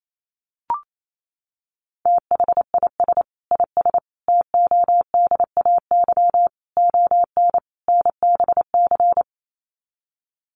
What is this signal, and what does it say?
Morse code, heard as a keyed tone, about 28 wpm, 700 Hz. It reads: THIS IS TODAY ON NBC
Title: Sets of 5 Words Encore 28wpm